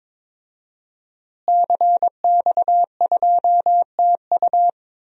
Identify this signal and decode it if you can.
Morse code, heard as a keyed tone, about 22 wpm, 700 Hz. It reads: CX2TU
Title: DXCC-100 Call Signs - 5 Characters - Repeated 2x - 22wpm